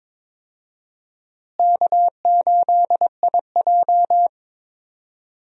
Morse code, heard as a keyed tone, about 21 wpm, 700 Hz. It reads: K8IJ
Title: USA Call Signs - 4 Characters - Repeated 3x - 22wpm